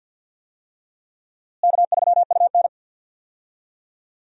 Morse code, heard as a keyed tone, about 52 wpm, 700 Hz. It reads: K3UN
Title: USA Call Signs - 4 Characters - Repeated 2x - 50wpm